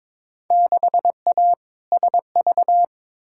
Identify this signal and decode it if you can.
Morse code, heard as a keyed tone, about 22 wpm, 700 Hz. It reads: 6A SV
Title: Field Day Exchanges Encore - 22wpm